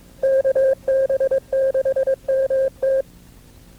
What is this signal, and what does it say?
Morse code, heard as a keyed tone, about 22 wpm, 530 Hz. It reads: KB6MT
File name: FreeFiles / CW-practice / KB6MT-tape2-side1